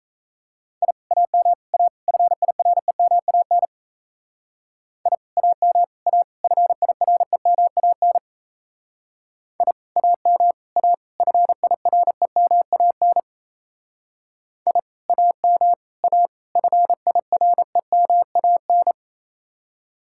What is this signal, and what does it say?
Morse code, heard as a keyed tone, about 36 wpm, 700 Hz. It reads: I AM A FIREMAN I AM A FIREMAN I AM A FIREMAN I AM A FIREMAN